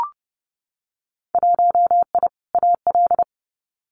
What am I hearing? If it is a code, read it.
Morse code, heard as a keyed tone, about 30 wpm, 700 Hz. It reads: 1I AL
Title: Winter Field Day Exchanges 30wpm